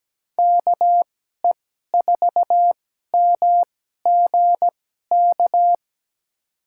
Morse code, heard as a keyed tone, about 17 wpm, spaced as for 12 wpm, 700 Hz. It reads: KE4MGK
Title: USA Call Signs - 6 Characters - 2x Character Spacing - 17wpm